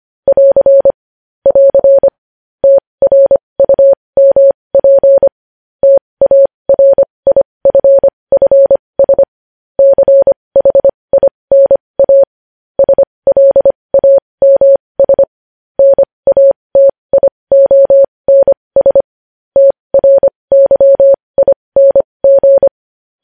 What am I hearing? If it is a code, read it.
Morse code, heard as a keyed tone, about 25 wpm, spaced as for 20 wpm, 550 Hz. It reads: <AR> <AR> TRUMP TARIFFS CHINA SLAMS NATIONS TRYING
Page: News Headlines in Morse Code at 20 WPM